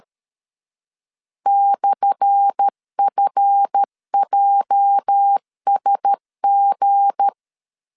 Morse code, heard as a keyed tone, about 13 wpm, 800 Hz. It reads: /FJSG
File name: LEZIONI / 10 SIGNS / 10A SLASH 12wpm 2314